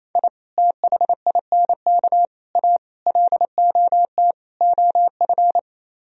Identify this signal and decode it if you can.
Morse code, heard as a keyed tone, about 35 wpm, 700 Hz. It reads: I THINK A LOT OF
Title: Speed-Racing - Sets of 5 Words 28wpm